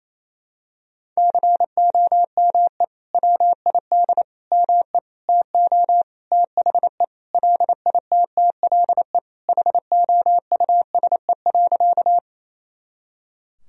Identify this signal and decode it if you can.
Morse code, heard as a keyed tone, about 28 wpm, 700 Hz. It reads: COME WID ME TO THE LITTLE HOUSE.
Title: Sentences from Top 200 Words and Abbreviations 28wpm